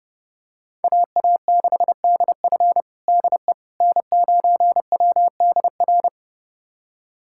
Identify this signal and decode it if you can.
Morse code, heard as a keyed tone, about 30 wpm, 700 Hz. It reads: AA6DF DE N9WDR
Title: USA Callsign DE Callsign - 5 Characters - 30wpm